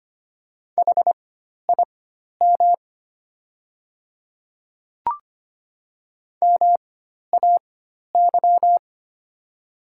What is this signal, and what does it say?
Morse code, heard as a keyed tone, about 25 wpm, spaced as for 12 wpm, 700 Hz. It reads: HIM MAY
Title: 4x Character Spacing - Top 100 Words - 25wpm